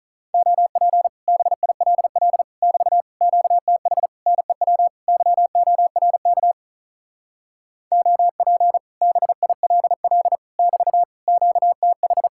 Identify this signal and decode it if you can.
Morse code, heard as a keyed tone, about 43 wpm, 700 Hz. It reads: OP BILL <BT> QTH NEW YORK OP BILL <BT> QTH
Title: Speed-Racing Common QSO Phrases - 35wpm